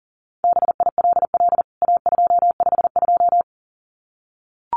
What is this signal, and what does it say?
Morse code, heard as a keyed tone, about 40 wpm, 700 Hz. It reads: BILL A252
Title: CWT Exchange 40wpm